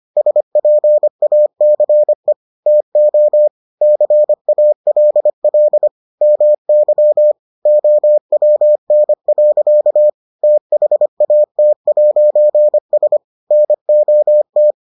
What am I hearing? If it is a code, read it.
Morse code, heard as a keyed tone, about 25 wpm, 600 Hz. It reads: SPACE TO CALL MY OWN. THAT'S NOT